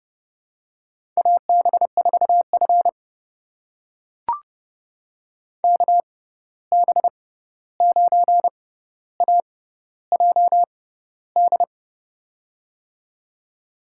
Morse code, heard as a keyed tone, about 30 wpm, spaced as for 14 wpm, 700 Hz. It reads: AB4F KB9AJD
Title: USA Call Signs - 6x Character Spacing - 30wpm